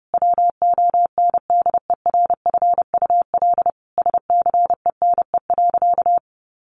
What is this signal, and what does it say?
Morse code, heard as a keyed tone, about 28 wpm, 700 Hz. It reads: WONDERFUL SCENE.
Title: Sentences from Top 1100 Words 30wpm